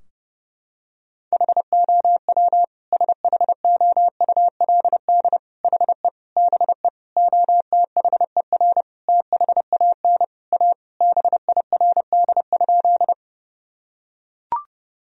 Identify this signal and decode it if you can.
Morse code, heard as a keyed tone, about 30 wpm, 700 Hz. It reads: HOW SHOULD HE BE OTHER THAN A BIRD?
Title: Sentences from Top 1000 Words - Review Part 2 - 30wpm